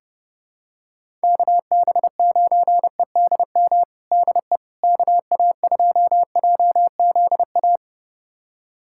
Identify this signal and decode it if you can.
Morse code, heard as a keyed tone, about 32 wpm, 700 Hz. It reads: KB9EDM DE KA2JZA
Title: USA Callsign DE Callsign 30wpm